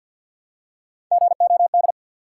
Morse code, heard as a keyed tone, about 50 wpm, 700 Hz. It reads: GOD
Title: Three Letter Words - 50wpm